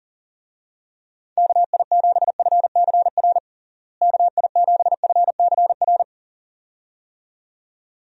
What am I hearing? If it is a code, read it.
Morse code, heard as a keyed tone, about 37 wpm, 700 Hz. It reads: KI7FCR KI7FCR
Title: USA Call Signs - 6 Characters - Repeated 2x - 40wpm